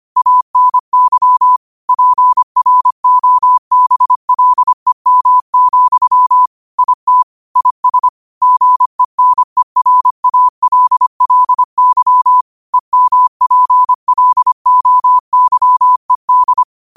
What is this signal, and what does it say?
Morse code, heard as a keyed tone, about 25 wpm, 1000 Hz. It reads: ANY PROBLEM, IT IS GENERALLY EMPLOYED
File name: qotdfiles / 25wpm